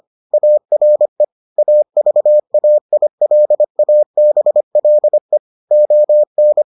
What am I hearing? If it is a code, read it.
Morse code, heard as a keyed tone, about 25 wpm, 600 Hz. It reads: ARE AVAILABLE ON